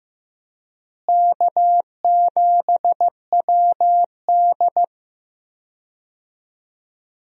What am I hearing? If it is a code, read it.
Morse code, heard as a keyed tone, about 15 wpm, 700 Hz. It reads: K7WD
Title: USA Call Signs - 4 Characters - Repeated 3x - 15wpm